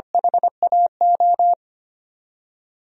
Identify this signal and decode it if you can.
Morse code, heard as a keyed tone, about 25 wpm, 700 Hz. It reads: HAO
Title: DXCC-100 Call Signs - 6 Characters - 25wpm